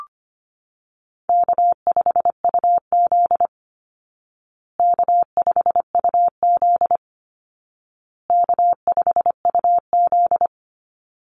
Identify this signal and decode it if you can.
Morse code, heard as a keyed tone, about 25 wpm, 700 Hz. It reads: K5UZ K5UZ K5UZ
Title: Easy Call Signs Repeated 3x - 25wpm